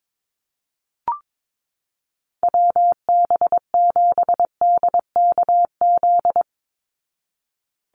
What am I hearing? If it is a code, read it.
Morse code, heard as a keyed tone, about 22 wpm, 700 Hz. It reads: WB7DKZ
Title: Call Signs Repeated 3x - 22wpm